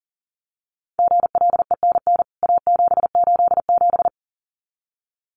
Morse code, heard as a keyed tone, about 40 wpm, 700 Hz. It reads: GLENN A787
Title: CWT Exchange 40wpm